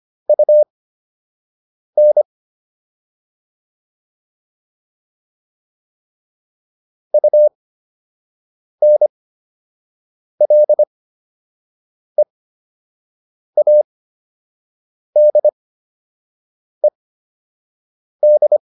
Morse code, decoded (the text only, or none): UN UNLEADED